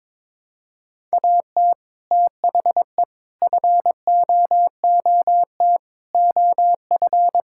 Morse code, heard as a keyed tone, about 21 wpm, 700 Hz. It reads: AT THE FOOT OF
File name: practice / 22wpm / Build A Phrase - 5-Word Phrases - 22wpm